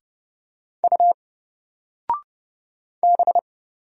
Morse code, heard as a keyed tone, about 30 wpm, 700 Hz. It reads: U B
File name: Single-Letter-Number-30wpm